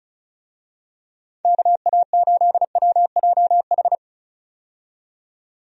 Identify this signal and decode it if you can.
Morse code, heard as a keyed tone, about 34 wpm, 700 Hz. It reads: KA8WJH
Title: USA Call Signs - 6 Characters - 35wpm